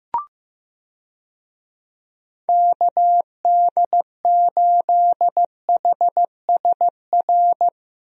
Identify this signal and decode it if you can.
Morse code, heard as a keyed tone, about 16 wpm, 700 Hz. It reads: KD8HSR
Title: USA Call Signs - 6 Characters - Repeated 2x - 15wpm